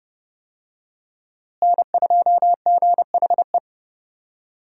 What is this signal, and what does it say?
Morse code, heard as a keyed tone, about 30 wpm, 700 Hz. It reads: N2GHE
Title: USA Call Signs - 5 Characters - Repeated 3x - 30wpm